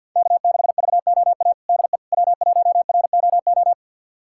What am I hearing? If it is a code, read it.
Morse code, heard as a keyed tone, about 50 wpm, 700 Hz. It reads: K6VOA DE W1ROO
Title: USA Callsign DE Callsign - 5 Characters - 50wpm